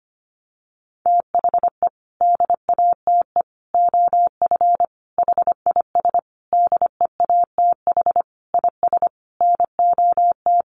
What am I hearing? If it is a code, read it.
Morse code, heard as a keyed tone, about 25 wpm, 700 Hz. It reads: THE DATE OF HIS DEATH IS NOT
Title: Sentences from Top 1100 Words 25wpm